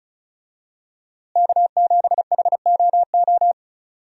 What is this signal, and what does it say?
Morse code, heard as a keyed tone, about 36 wpm, 700 Hz. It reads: K7HOO
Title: USA Call Signs - 5 Characters - Repeated 3x - 35wpm